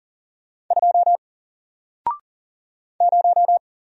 Morse code, heard as a keyed tone, about 40 wpm, 700 Hz. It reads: 2 0
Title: Single Letter-Number 40wpm